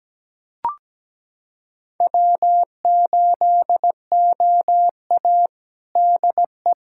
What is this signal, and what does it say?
Morse code, heard as a keyed tone, about 17 wpm, 700 Hz. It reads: W8OA DE
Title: USA Callsign DE Callsign - 4 Characters - 17wpm